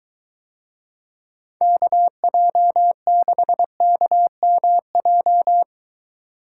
Morse code, heard as a keyed tone, about 23 wpm, 700 Hz. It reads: KJ6KMJ